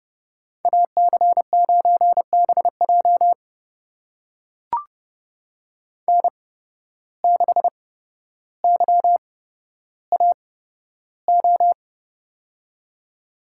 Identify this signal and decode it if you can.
Morse code, heard as a keyed tone, about 30 wpm, spaced as for 13 wpm, 700 Hz. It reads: AC9BJ N6YAO
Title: USA Call Signs - 5 Characters - 8x Character Spacing - 30wpm